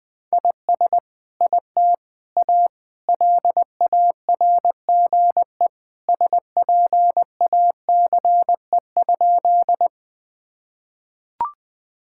Morse code, heard as a keyed tone, about 21 wpm, 700 Hz. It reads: IS IT A LARGE SPACE?
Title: Sentences from Top 1000 Words - Review Part 5 - 20wpm